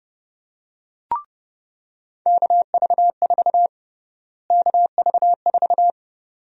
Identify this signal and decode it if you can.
Morse code, heard as a keyed tone, about 30 wpm, 700 Hz. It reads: KV4 KV4
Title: USA Call Sign Prefixes - Repeated 2x - 30wpm